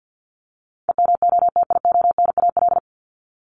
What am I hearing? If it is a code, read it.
Morse code, heard as a keyed tone, about 51 wpm, 700 Hz. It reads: EMOTIONAL
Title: Top 501-750 Words 50wpm